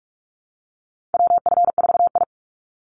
Morse code, heard as a keyed tone, about 44 wpm, 700 Hz. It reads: WF4I
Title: NA SOTA Chaser Callsigns 45wpm